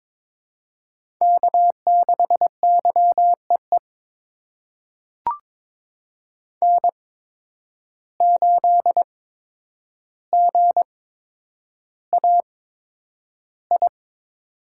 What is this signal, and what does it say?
Morse code, heard as a keyed tone, about 22 wpm, spaced as for 9 wpm, 700 Hz. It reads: K6YEE N8GAI
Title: USA Call Signs - 5 Characters - 8x Character Spacing - 22wpm